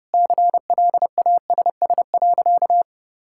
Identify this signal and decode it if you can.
Morse code, heard as a keyed tone, about 32 wpm, 700 Hz. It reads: CLASS.
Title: Sentences from Top 1000 Words - Review Part 6 - 30wpm